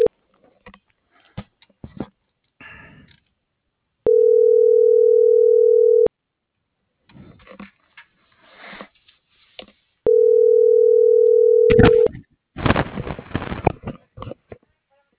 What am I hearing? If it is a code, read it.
Ringback tone (North America), 440 + 480 Hz.
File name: CENTRAL MABE / out edwin.sanchez